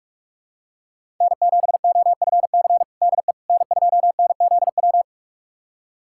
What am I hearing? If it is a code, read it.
Morse code, heard as a keyed tone, about 47 wpm, 700 Hz. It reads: N7OFC DE N2NZW
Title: USA Callsign DE Callsign - 5 Characters - 45wpm